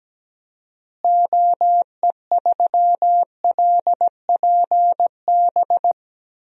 Morse code, heard as a keyed tone, about 17 wpm, 700 Hz. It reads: OE3LPB
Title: DXCC-100 Call Signs - 6 Characters - Repeated 2x - 17wpm